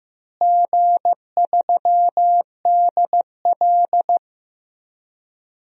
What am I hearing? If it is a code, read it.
Morse code, heard as a keyed tone, about 15 wpm, 700 Hz. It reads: G3DL
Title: DXCC-100 Call Signs - 4 Characters - Repeated 3x - 15wpm